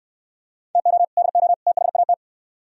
Morse code, heard as a keyed tone, about 65 wpm, 700 Hz. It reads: TO DO THAT